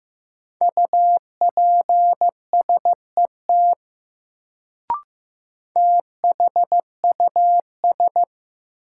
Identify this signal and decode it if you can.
Morse code, heard as a keyed tone, about 15 wpm, 700 Hz. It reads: UPSET THUS
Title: Sentences from Top 800 Words - New Word Drill - 15wpm